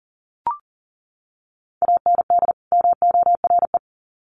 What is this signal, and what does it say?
Morse code, heard as a keyed tone, about 40 wpm, 700 Hz. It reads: AND MORE
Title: Sets of 2 Words 40wpm